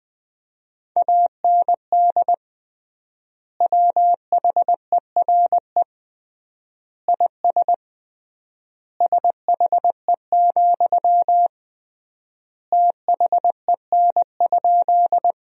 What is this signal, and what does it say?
Morse code, heard as a keyed tone, about 20 wpm, 700 Hz. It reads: AND WHERE IS SHE, THEN?
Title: 3x Word-Spacing - Sentences from Top 200 Words - 20wpm